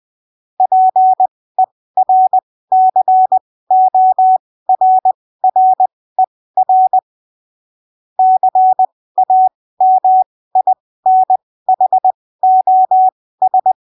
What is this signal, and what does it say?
Morse code, heard as a keyed tone, about 20 wpm, spaced as for 15 wpm, 750 Hz. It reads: PERCORRER CAMINHOS
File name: arquivos / texto / texto 19 20ppm